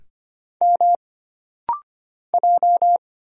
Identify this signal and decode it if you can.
Morse code, heard as a keyed tone, about 25 wpm, 700 Hz. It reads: M J
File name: Single Character (Letter:Number:Punctuation:Prosign) - Rapid-Fire - 25wpm